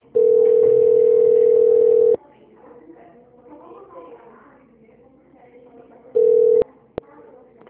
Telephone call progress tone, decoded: ringback tone

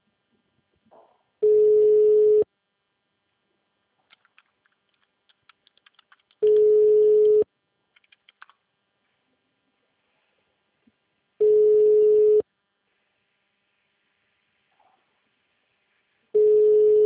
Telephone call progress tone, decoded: ringback tone

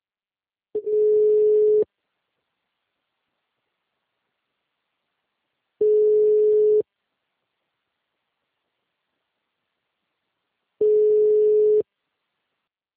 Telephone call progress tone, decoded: ringback tone

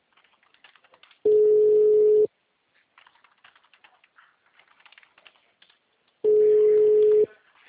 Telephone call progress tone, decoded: ringback tone

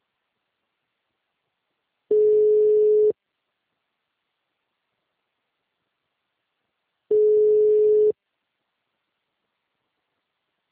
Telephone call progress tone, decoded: ringback tone

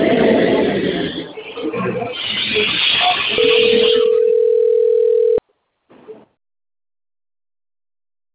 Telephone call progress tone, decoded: ringback tone

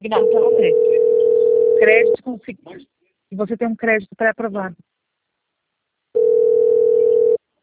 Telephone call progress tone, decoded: ringback tone

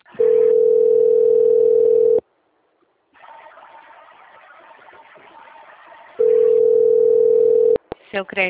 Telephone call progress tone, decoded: ringback tone

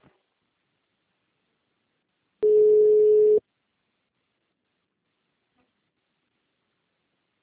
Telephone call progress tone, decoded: ringback tone

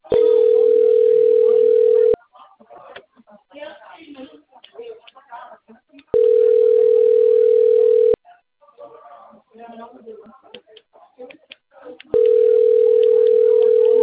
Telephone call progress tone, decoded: ringback tone

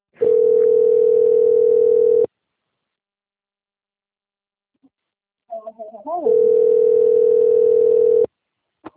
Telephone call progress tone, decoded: ringback tone

